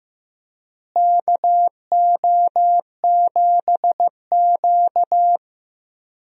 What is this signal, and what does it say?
Morse code, heard as a keyed tone, about 15 wpm, 700 Hz. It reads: KO7Q
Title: USA Call Signs - 4 Characters - Repeated 2x - 15wpm